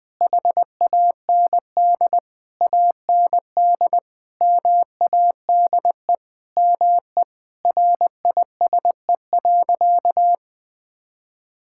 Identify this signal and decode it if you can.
Morse code, heard as a keyed tone, about 20 wpm, 700 Hz. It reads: HAND AND MADE ME RISE.
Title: Sentences from Top 1000 Words - Review Part 6 - 20wpm